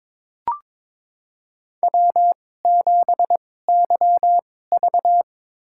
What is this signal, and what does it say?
Morse code, heard as a keyed tone, about 22 wpm, spaced as for 16 wpm, 700 Hz. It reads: W7YV
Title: USA Call Signs - 4 Characters - 2x Character Spacing - 22wpm